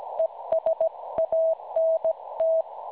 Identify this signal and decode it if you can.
Morse code, heard as a keyed tone, about 17 wpm, 650 Hz. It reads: ES ANT
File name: cwronde / websdr recording .6kHz